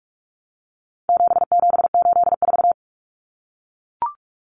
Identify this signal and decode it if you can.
Morse code, heard as a keyed tone, about 44 wpm, 700 Hz. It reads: 7784
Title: Common Cut Numbers - 4-Digit 45wpm